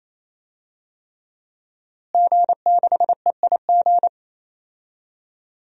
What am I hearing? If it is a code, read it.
Morse code, heard as a keyed tone, about 28 wpm, 700 Hz. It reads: G6EIG
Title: DXCC-100 Call Signs - 5 Characters - Repeated 3x - 28wpm